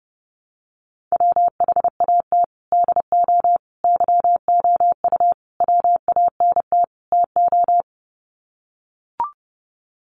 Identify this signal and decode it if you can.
Morse code, heard as a keyed tone, about 30 wpm, 700 Hz. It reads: WHAT DO YOU WANT TO
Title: Sets of 5 Words 30wpm